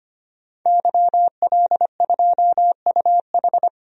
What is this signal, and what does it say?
Morse code, heard as a keyed tone, about 25 wpm, 690 Hz. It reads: YL2UH
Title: DXCC-100 Call Signs - 5 Characters - 25wpm